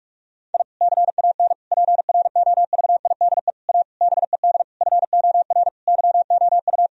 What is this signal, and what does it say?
Morse code, heard as a keyed tone, about 41 wpm, 700 Hz. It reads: I CAN PROVIDE A BED FOR YOU
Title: Sentences from Top 1000 Words - Review Part 3 - 45wpm